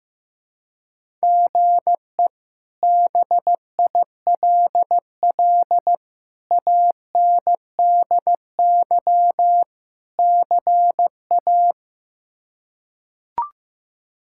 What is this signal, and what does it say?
Morse code, heard as a keyed tone, about 15 wpm, 700 Hz. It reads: GE BILL ANDY CA
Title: K1USN SST QSO Practice - 15wpm